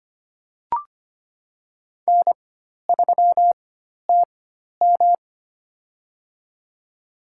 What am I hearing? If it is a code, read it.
Morse code, heard as a keyed tone, about 25 wpm, spaced as for 12 wpm, 700 Hz. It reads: N3TM
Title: USA Call Signs - 4 Characters - 4x Character Spacing - 25wpm